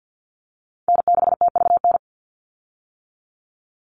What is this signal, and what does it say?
Morse code, heard as a keyed tone, about 50 wpm, 700 Hz. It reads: N6TVN
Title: Call Signs - 50wpm